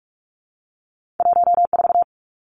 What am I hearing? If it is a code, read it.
Morse code, heard as a keyed tone, about 46 wpm, 700 Hz. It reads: 14
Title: Numbers - 2-Digit 45wpm